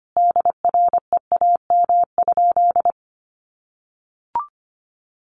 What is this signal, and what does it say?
Morse code, heard as a keyed tone, about 25 wpm, 700 Hz. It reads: DREAM?
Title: Sentences from Top 1500 Words 25wpm